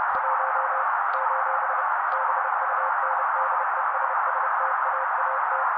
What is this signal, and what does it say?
Morse code, heard as a keyed tone, about 30 wpm, 530 Hz. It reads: 1 : BUNDESSTAAT
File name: daily / 29wpm noise